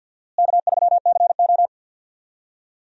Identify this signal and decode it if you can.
Morse code, heard as a keyed tone, about 52 wpm, 700 Hz. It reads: K3CO
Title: USA Call Signs - 4 Characters - Repeated 2x - 50wpm